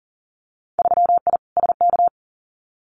Morse code, heard as a keyed tone, about 41 wpm, 700 Hz. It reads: 3I SK
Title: Winter Field Day Exchanges Encore 40wpm